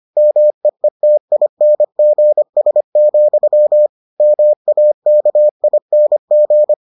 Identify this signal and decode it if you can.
Morse code, heard as a keyed tone, about 25 wpm, 600 Hz. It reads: MEETINGS, MAKING